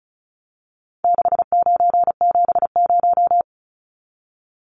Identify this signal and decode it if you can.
Morse code, heard as a keyed tone, about 34 wpm, 700 Hz. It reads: 6970
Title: Numbers - 4-Digit 35wpm